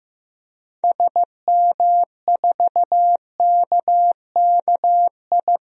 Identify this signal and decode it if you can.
Morse code, heard as a keyed tone, about 15 wpm, 700 Hz. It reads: SM4KKI
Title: DXCC-100 Call Signs - 6 Characters - 15wpm